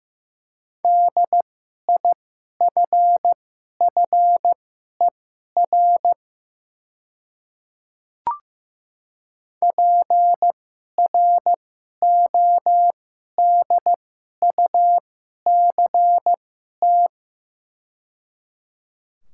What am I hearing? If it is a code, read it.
Morse code, heard as a keyed tone, about 15 wpm, spaced as for 11 wpm, 700 Hz. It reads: DIFFER PRODUCT
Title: 2x Character Spacing - Top 500 Words - 15wpm